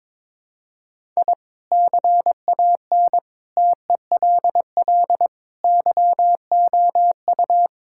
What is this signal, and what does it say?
Morse code, heard as a keyed tone, about 22 wpm, 700 Hz. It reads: I CAN TELL YOU
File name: practice / 22wpm / Sentences with 4-Word Phrases - 22wpm